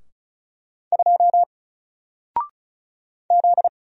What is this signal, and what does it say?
Morse code, heard as a keyed tone, about 36 wpm, 700 Hz. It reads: 2 Z
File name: Single-Letter-Number-35wpm